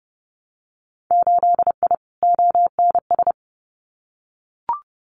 Morse code, heard as a keyed tone, about 30 wpm, 700 Hz. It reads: 8I ONS
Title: Winter Field Day Exchanges Encore 30wpm